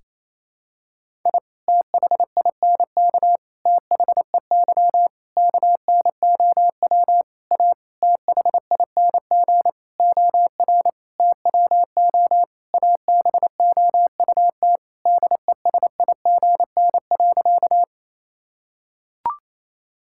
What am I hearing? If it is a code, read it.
Morse code, heard as a keyed tone, about 28 wpm, 700 Hz. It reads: I THINK THEY KNOW A THING OR TWO ABOUT DESIGN.
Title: Sentences from Top 1000 Words - Review Part 5 - 28wpm